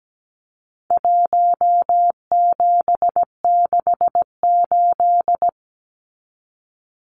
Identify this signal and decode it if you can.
Morse code, heard as a keyed tone, about 17 wpm, 700 Hz. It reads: 1768
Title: Numbers - Contest Style - 4-Digit 17wpm